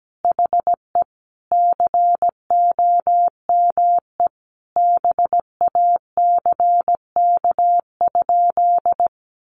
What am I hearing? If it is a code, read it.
Morse code, heard as a keyed tone, about 17 wpm, 700 Hz. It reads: HE COME BACK?
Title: Sentences from Top 200 Words 17wpm